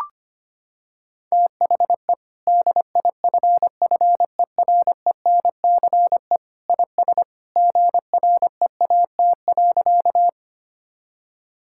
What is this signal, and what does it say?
Morse code, heard as a keyed tone, about 25 wpm, 700 Hz. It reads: THE DIFFERENCE IS GREAT.
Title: Sentences from Top 1000 Words - Review Part 5 - 25wpm